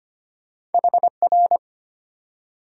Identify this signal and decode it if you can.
Morse code, heard as a keyed tone, about 33 wpm, 700 Hz. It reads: HR